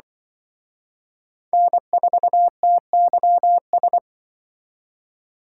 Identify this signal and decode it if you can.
Morse code, heard as a keyed tone, about 24 wpm, 700 Hz. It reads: N4TYS
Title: Speed Racing - USA 1x3 Call Signs 20wpm